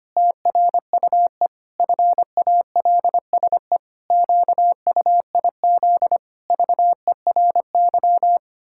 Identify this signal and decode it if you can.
Morse code, heard as a keyed tone, about 25 wpm, 700 Hz. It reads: TRUE FALSE QUIZ VERY
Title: Pangram Challenge v2 - 25wpm